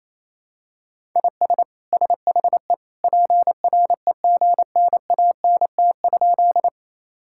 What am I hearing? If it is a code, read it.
Morse code, heard as a keyed tone, about 28 wpm, 700 Hz. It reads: IS SHE PREGNANT?
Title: Sentences from Top 1000 Words - Review Part 6 - 28wpm